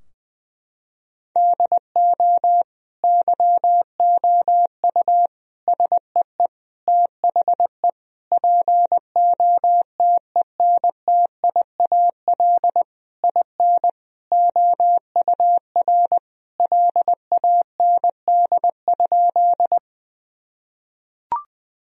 Morse code, heard as a keyed tone, about 21 wpm, 700 Hz. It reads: DO YOU SEE THE POTENTIAL IN OUR LAND?
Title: Sentences from Top 1000 Words - Review Part 4 - 20wpm